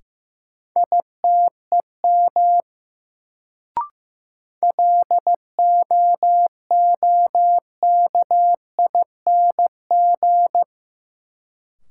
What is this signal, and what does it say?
Morse code, heard as a keyed tone, about 15 wpm, 700 Hz. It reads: ITEM LOOKING
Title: Sentences from Top 1000 Words - Review Part 6 - New Word Drill - 15wpm